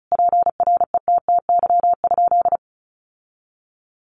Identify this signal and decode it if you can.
Morse code, heard as a keyed tone, about 34 wpm, 700 Hz. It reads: PRETTY?
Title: Sentences from Top 1000 Words 35wpm